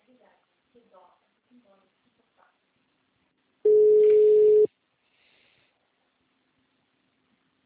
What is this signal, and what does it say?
Ringback tone (Europe, 425 Hz), with speech.
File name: out- -2024- - - .8